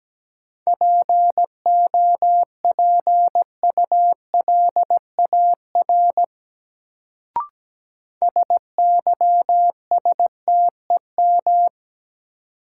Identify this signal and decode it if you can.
Morse code, heard as a keyed tone, about 17 wpm, 700 Hz. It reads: POPULAR SYSTEM
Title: Sentences from Top 1000 Words - Review Part 3 - New Word Drill - 17wpm